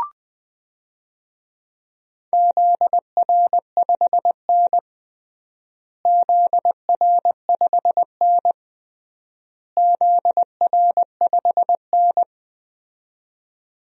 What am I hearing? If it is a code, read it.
Morse code, heard as a keyed tone, about 20 wpm, 700 Hz. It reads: ZR5N ZR5N ZR5N
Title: DXCC-100 Call Signs - 4 Characters - Repeated 3x - 20wpm